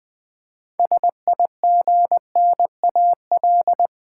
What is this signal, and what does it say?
Morse code, heard as a keyed tone, about 20 wpm, 700 Hz. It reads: SIGNAL
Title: Six Letter Words - 20wpm